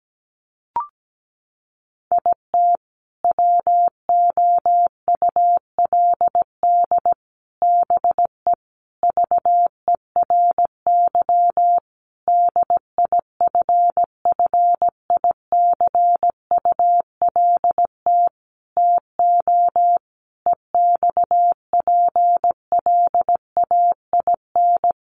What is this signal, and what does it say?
Morse code, heard as a keyed tone, about 17 wpm, 700 Hz. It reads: IT WOULD BE VERY DIFFICULT TO EXPLAIN
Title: Sentences from Top 1100 Words 17wpm